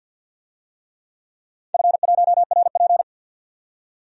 Morse code, heard as a keyed tone, about 52 wpm, 700 Hz. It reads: F1RP
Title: DXCC-100 Call Signs - 4 Characters - 50wpm